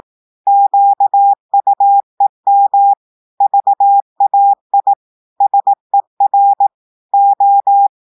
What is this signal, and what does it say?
Morse code, heard as a keyed tone, about 18 wpm, 800 Hz. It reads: QUEM VAI SER O
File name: audios / portugais / portugais texte en clair 2 vitesse 18 mots minutes